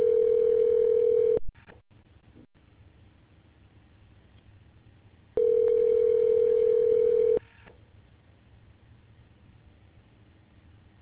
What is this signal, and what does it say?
Ringback tone (North America), 440 + 480 Hz.